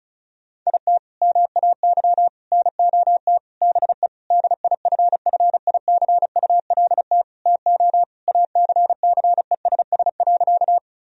Morse code, heard as a keyed tone, about 36 wpm, 700 Hz. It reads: IT MAY NOT BE DIFFICULT TO ACCESS.
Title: Sentences from Top 1000 Words - Review Part 4 - 35wpm